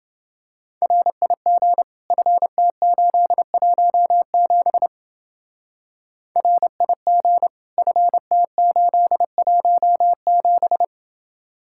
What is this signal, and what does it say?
Morse code, heard as a keyed tone, about 30 wpm, 700 Hz. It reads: RIG FT817 RIG FT817